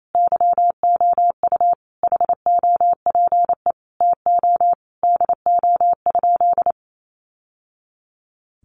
Morse code, heard as a keyed tone, about 28 wpm, 700 Hz. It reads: YOU HOPE TO DO?
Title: Sentences from Top 1000 Words 28wpm